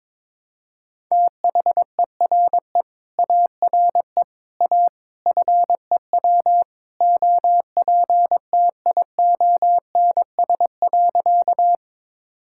Morse code, heard as a keyed tone, about 22 wpm, 700 Hz. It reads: THERE ARE A FEW OPTIONS.
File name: practice / 22wpm / Sentences with 3-Word Phrases - 22wpm